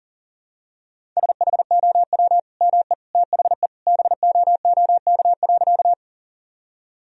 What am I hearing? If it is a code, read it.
Morse code, heard as a keyed tone, about 40 wpm, 700 Hz. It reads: SHOW ME THE BOOK.
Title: Sentences from Top 1000 Words - Review Part 1 - 40wpm